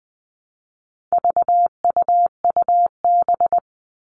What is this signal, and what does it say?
Morse code, heard as a keyed tone, about 20 wpm, 700 Hz. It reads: VUUB
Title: All Cut Numbers - 4-Digit 20wpm